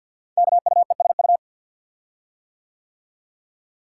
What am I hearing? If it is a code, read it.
Morse code, heard as a keyed tone, about 47 wpm, 700 Hz. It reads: KUESU
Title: Code Groups - 5 Letter:Numbers - 50wpm